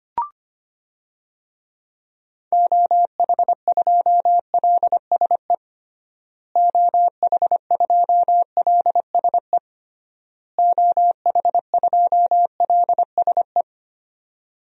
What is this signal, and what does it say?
Morse code, heard as a keyed tone, about 25 wpm, 700 Hz. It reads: OH2LSE OH2LSE OH2LSE
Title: DXCC-100 Call Signs - 6 Characters - Repeated 3x - 25wpm